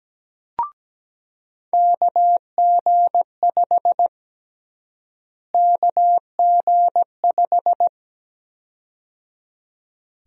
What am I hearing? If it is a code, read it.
Morse code, heard as a keyed tone, about 17 wpm, 700 Hz. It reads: KG5 KG5
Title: USA Call Sign Prefixes - Repeated 2x - 17wpm